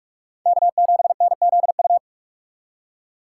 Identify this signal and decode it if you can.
Morse code, heard as a keyed tone, about 63 wpm, spaced as for 54 wpm, 700 Hz. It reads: K7NZU
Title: Speed Racing - USA 1x3 Call Signs 45wpm